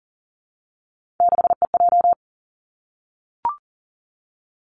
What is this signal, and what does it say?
Morse code, heard as a keyed tone, about 40 wpm, 700 Hz. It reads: 6EJ
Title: Three Challenging Letter-Number Combinations 40wpm